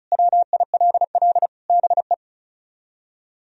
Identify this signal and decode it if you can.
Morse code, heard as a keyed tone, about 43 wpm, 700 Hz. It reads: WILL BE
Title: Speed-Racing - Sets of 3 Words 35wpm